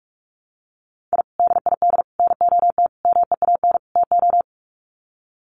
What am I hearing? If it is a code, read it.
Morse code, heard as a keyed tone, about 44 wpm, 700 Hz. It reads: I DID NOT MEAN TO
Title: Sets of 5 Words 45wpm